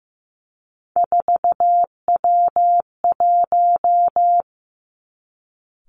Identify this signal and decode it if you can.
Morse code, heard as a keyed tone, about 15 wpm, 700 Hz. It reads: 4W1
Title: Three Challenging Letter-Number Combinations 15wpm